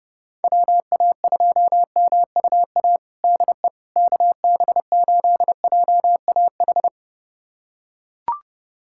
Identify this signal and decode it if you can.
Morse code, heard as a keyed tone, about 30 wpm, 700 Hz. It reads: WA2MUA DE KB8JAH
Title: USA Callsign DE Callsign 30wpm